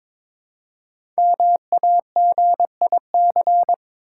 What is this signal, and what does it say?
Morse code, heard as a keyed tone, about 23 wpm, 690 Hz. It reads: MAGIC
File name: 5 Letter Words - First 3 Letters Given - 22wpm